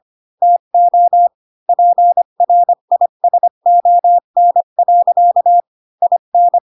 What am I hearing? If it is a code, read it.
Morse code, heard as a keyed tone, about 25 wpm, 700 Hz. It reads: TO PRISON. IN
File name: Pinocchio / Pinocchio ch4 20 25 ik1hge